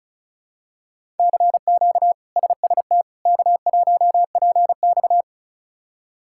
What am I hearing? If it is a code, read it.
Morse code, heard as a keyed tone, about 34 wpm, 700 Hz. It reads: CQ SST K1PX
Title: K1USN SST QSO Practice - 35wpm